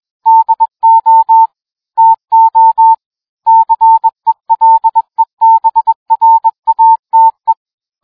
Morse code, heard as a keyed tone, about 21 wpm, 900 Hz. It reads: DO TO CELEBRATE